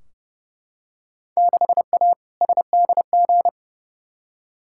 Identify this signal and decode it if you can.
Morse code, heard as a keyed tone, about 30 wpm, 700 Hz. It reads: 6A SDG
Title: Field Day Exchanges Encore - 30wpm